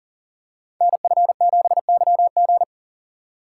Morse code, heard as a keyed tone, about 42 wpm, 700 Hz. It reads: NF7YG
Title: Code Groups - 5 Letter:Numbers - 40wpm